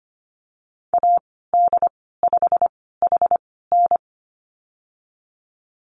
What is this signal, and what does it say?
Morse code, heard as a keyed tone, about 25 wpm, spaced as for 16 wpm, 700 Hz. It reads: AD5HN
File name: Lesson 15g - Number 5 - Call Signs 25wpm